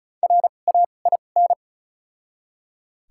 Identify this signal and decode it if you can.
Morse code, heard as a keyed tone, about 36 wpm, spaced as for 25 wpm, 700 Hz. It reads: RAIN
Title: 2x Character Spacing - Top 300 Words - 35wpm